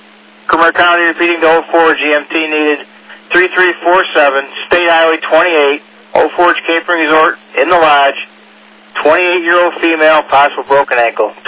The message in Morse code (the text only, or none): none